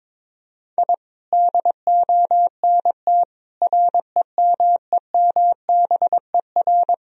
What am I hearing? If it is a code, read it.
Morse code, heard as a keyed tone, about 22 wpm, 700 Hz. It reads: I DONT REMEMBER
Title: Sentences from Top 1000 Words - Review Part 1 - 22wpm